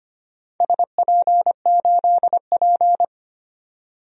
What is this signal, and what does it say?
Morse code, heard as a keyed tone, about 25 wpm, 700 Hz. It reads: SP8P
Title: DXCC-100 Call Signs - 4 Characters - 25wpm